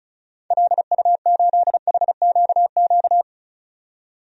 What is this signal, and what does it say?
Morse code, heard as a keyed tone, about 36 wpm, 700 Hz. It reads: LU8HQQ
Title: DXCC-100 Call Signs - 6 Characters - 35wpm